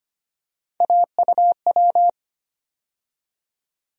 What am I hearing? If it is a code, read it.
Morse code, heard as a keyed tone, about 25 wpm, 700 Hz. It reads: AUW
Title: USA Call Sign Postfixes - 3 Letters 25wpm